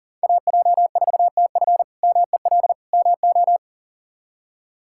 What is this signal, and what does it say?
Morse code, heard as a keyed tone, about 40 wpm, 700 Hz. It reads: AJ4TF MEL MO
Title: K1USN SST QSO Practice - 40wpm